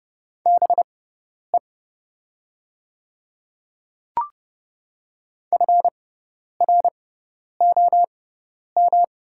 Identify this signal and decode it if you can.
Morse code, heard as a keyed tone, about 30 wpm, spaced as for 10 wpm, 700 Hz. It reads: BE FROM
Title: 6x Character Spacing - Top 200 Words - 30wpm